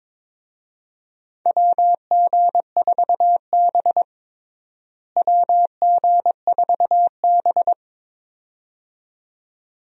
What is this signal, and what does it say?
Morse code, heard as a keyed tone, about 22 wpm, 700 Hz. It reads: WG4B WG4B
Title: USA Call Signs - 4 Characters - Repeated 2x - 22wpm